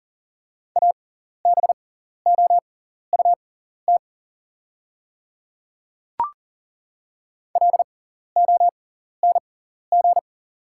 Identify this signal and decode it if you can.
Morse code, heard as a keyed tone, about 38 wpm, spaced as for 14 wpm, 690 Hz. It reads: ABOUT LONG
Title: 6x Character Spacing - Top 100 Words - 40wpm